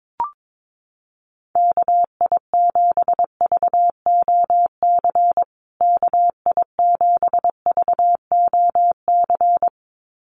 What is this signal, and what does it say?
Morse code, heard as a keyed tone, about 22 wpm, 700 Hz. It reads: KI7VOC KI7VOC
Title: USA Call Signs Repeated 2x 22wpm